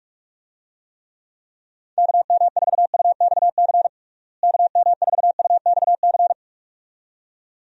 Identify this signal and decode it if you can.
Morse code, heard as a keyed tone, about 46 wpm, 700 Hz. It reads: KM4UXC KM4UXC
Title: USA Call Signs - 6 Characters - Repeated 2x - 45wpm